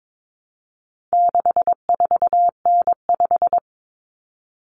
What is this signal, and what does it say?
Morse code, heard as a keyed tone, about 22 wpm, 700 Hz. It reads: 64N5
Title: Common Cut Numbers - 4-Digit 22wpm